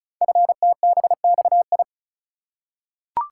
Morse code, heard as a keyed tone, about 36 wpm, 690 Hz. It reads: FTBXI
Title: Code Groups - 5 Letters - 35wpm